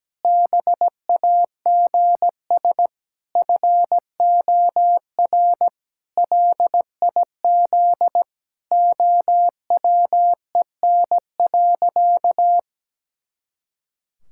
Morse code, heard as a keyed tone, about 17 wpm, 700 Hz. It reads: BAGS FOR LIZ OWEN.
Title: Pangram Challenge v2 - 17wpm